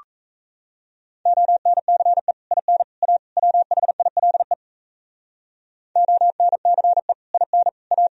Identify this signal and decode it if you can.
Morse code, heard as a keyed tone, about 40 wpm, 700 Hz. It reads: ONCE IN A WHILE ONCE IN A